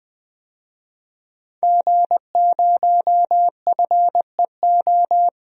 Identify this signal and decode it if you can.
Morse code, heard as a keyed tone, about 20 wpm, 700 Hz. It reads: G0FEO
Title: DXCC-100 Call Signs - 5 Characters - 20wpm